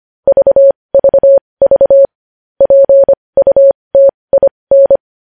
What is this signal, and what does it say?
Morse code, heard as a keyed tone, about 25 wpm, spaced as for 20 wpm, 550 Hz. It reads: VVV PUTIN
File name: podcast / News headlines-1x-25x20